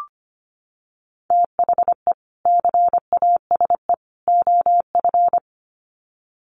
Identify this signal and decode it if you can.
Morse code, heard as a keyed tone, about 25 wpm, 700 Hz. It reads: THE CASE OF
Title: Sets of 3 Words 25wpm